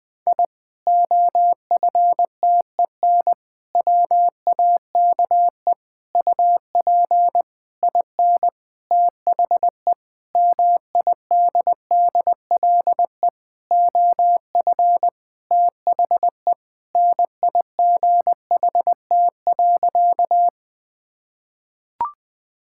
Morse code, heard as a keyed tone, about 21 wpm, 700 Hz. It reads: I OFTEN WAKE UP IN THE MIDDLE OF THE NIGHT.
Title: Sentences with 8-Word Phrases - 20wpm